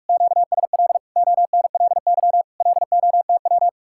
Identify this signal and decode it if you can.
Morse code, heard as a keyed tone, about 45 wpm, 700 Hz. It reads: QSL ONLY LOTW